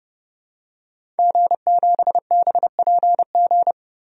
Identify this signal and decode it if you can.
Morse code, heard as a keyed tone, about 30 wpm, 700 Hz. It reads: G7BPG
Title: DXCC-100 Call Signs - 5 Characters - Repeated 2x - 30wpm